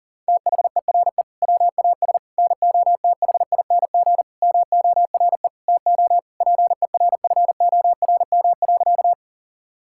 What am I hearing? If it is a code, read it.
Morse code, heard as a keyed tone, about 40 wpm, 700 Hz. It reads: THERE WAS NOTHING MORE TO PERFORM.
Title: Sentences from Top 1000 Words - Review Part 5 - 40wpm